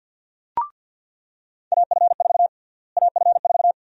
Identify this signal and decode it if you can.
Morse code, heard as a keyed tone, about 45 wpm, 700 Hz. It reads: AF4 AF4
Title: USA Call Sign Prefixes - Repeated 2x - 50wpm